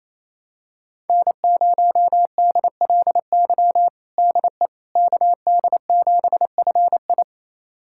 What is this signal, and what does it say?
Morse code, heard as a keyed tone, about 28 wpm, 700 Hz. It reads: N0DLY DE KD7FI
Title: USA Callsign DE Callsign - 5 Characters - 28wpm